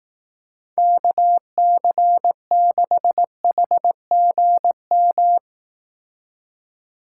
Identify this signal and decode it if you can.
Morse code, heard as a keyed tone, about 18 wpm, 700 Hz. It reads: KC6HGM
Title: Speed Racing - USA 2x3 Call Signs 15wpm